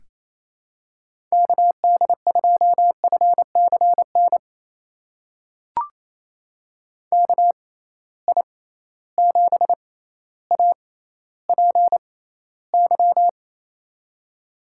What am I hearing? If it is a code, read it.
Morse code, heard as a keyed tone, about 28 wpm, spaced as for 14 wpm, 700 Hz. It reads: KD2FCN KI7APY
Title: USA Call Signs - 6x Character Spacing - 28wpm